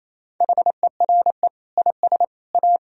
Morse code, heard as a keyed tone, about 28 wpm, 700 Hz. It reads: HERE IS A
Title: Sentences from Top 1000 Words - Review Part 1 - 28wpm